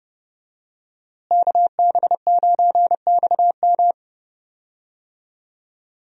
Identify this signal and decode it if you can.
Morse code, heard as a keyed tone, about 30 wpm, 700 Hz. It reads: KB9XM
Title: USA Call Signs - 5 Characters - Repeated 3x - 30wpm